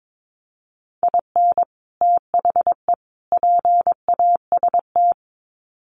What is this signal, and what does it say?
Morse code, heard as a keyed tone, about 22 wpm, 700 Hz. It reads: IN THE PAST
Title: Sets of 3 Words 22wpm